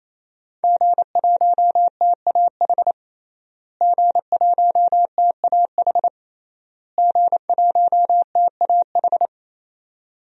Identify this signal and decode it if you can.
Morse code, heard as a keyed tone, about 28 wpm, 700 Hz. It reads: G1TAH G1TAH G1TAH
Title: DXCC-100 Call Signs - 5 Characters - Repeated 3x - 28wpm